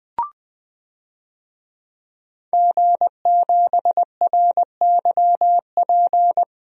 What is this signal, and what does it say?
Morse code, heard as a keyed tone, about 20 wpm, 700 Hz. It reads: G7RYP
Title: DXCC-100 Call Signs - 5 Characters - 20wpm